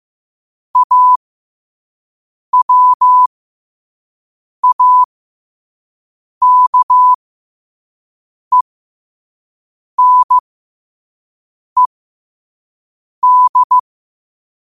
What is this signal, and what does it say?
Morse code, heard as a keyed tone, about 15 wpm, spaced as for 5 wpm, 1000 Hz. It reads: AWAKENED
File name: qotdfiles / 5wpm